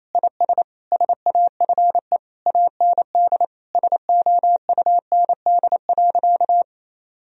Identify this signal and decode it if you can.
Morse code, heard as a keyed tone, about 28 wpm, 700 Hz. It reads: IS SAFE AND SOUND.
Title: Sentences from Top 1000 Words - Review Part 1 - 28wpm